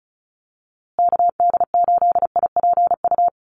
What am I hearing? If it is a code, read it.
Morse code, heard as a keyed tone, about 34 wpm, 700 Hz. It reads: KD8IPU
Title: USA Call Signs Repeated 2x 35wpm